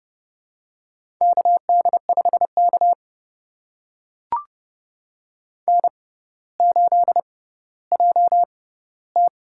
Morse code, heard as a keyed tone, about 30 wpm, spaced as for 15 wpm, 700 Hz. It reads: KD5K N8JT
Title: USA Call Signs - 4 Characters - 6x Character Spacing - 30wpm